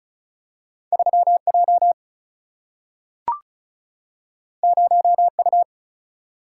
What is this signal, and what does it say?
Morse code, heard as a keyed tone, about 36 wpm, 700 Hz. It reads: 3J 0U
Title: Single-Letter Single-Number 35wpm